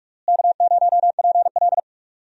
Morse code, heard as a keyed tone, about 41 wpm, 700 Hz. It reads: K0PL
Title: USA Call Signs - 4 Characters - Repeated 3x - 45wpm